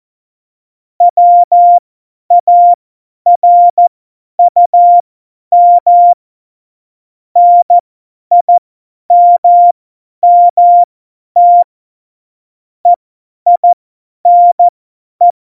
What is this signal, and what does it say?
Morse code, heard as a keyed tone, about 14 wpm, spaced as for 10 wpm, 700 Hz. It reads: WARUM NIMMT EINE
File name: Morsekurs / Witze / Witz 25 Juli 17, Tempo 50 F70